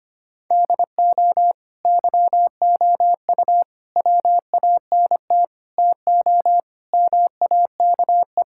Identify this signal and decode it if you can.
Morse code, heard as a keyed tone, about 25 wpm, 700 Hz. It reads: DO YOU WANT TO MAKE
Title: Sentences from Top 1000 Words - Review Part 6 - 25wpm